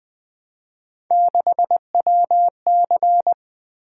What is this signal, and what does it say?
Morse code, heard as a keyed tone, about 20 wpm, 700 Hz. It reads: 6WC
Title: Three Letter-Number Combinations - 20wpm